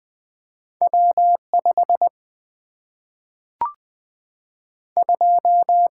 Morse code, heard as a keyed tone, about 20 wpm, 700 Hz. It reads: W 5 2
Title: Single-Letter Single-Number 20wpm